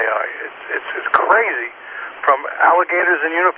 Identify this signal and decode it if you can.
Emergency Alert System (SAME) tones, heard as harsh buzzing bursts, with speech. It is heard from 0.6 to 3.3 s.